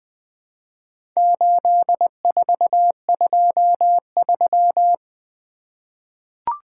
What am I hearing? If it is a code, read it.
Morse code, heard as a keyed tone, about 20 wpm, 700 Hz. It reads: 8423
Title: Four Letter-Number Combinations - 20wpm